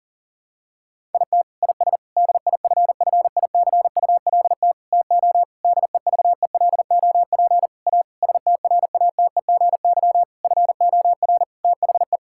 Morse code, heard as a keyed tone, about 40 wpm, 700 Hz. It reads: IT IS DIFFICULT TO DEVELOP A STRATEGY FOR THE
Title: Sentences from Top 1000 Words - Review Part 5 - 40wpm